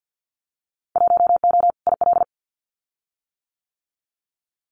Morse code, heard as a keyed tone, about 50 wpm, 700 Hz. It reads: 1O IL
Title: Winter Field Day Exchanges 50wpm